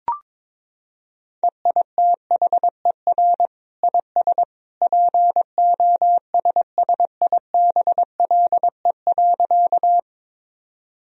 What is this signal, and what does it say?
Morse code, heard as a keyed tone, about 22 wpm, 700 Hz. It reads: EITHER IS POSSIBLE.
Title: Sentences from Top 1000 Words - Review Part 4 - 22wpm